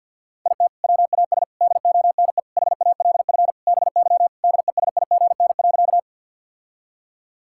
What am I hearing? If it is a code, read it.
Morse code, heard as a keyed tone, about 52 wpm, 700 Hz. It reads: IT WAS DONE HALF BY DESIGN.
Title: Sentences from Top 1000 Words - Review Part 5 - 50wpm